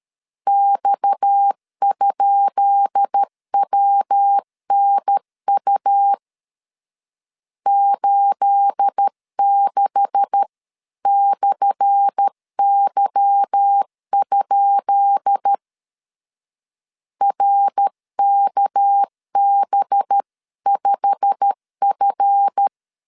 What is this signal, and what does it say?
Morse code, heard as a keyed tone, about 13 wpm, 800 Hz. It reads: X?WNU 86/Y? RKB5F
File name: LEZIONI / 10 SIGNS / 10B INT 12wpm 2414